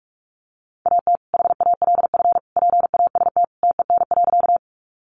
Morse code, heard as a keyed tone, about 44 wpm, 700 Hz. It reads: AT HALF PAST TEN.
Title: Sentences from Top 600 Words 45wpm